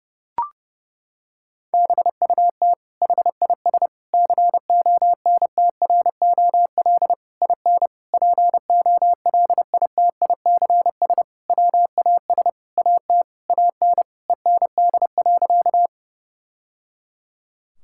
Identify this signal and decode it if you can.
Morse code, heard as a keyed tone, about 32 wpm, 700 Hz. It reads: BUT HIS CONTROL IN POLITICS WAS AT AN END.
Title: Sentences from Top 1000 Words - Review Part 6 - 30wpm